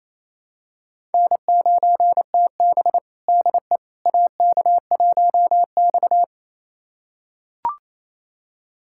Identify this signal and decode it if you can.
Morse code, heard as a keyed tone, about 28 wpm, 700 Hz. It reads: N9TB DE AK1X
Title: USA Callsign DE Callsign - 4 Characters - 28wpm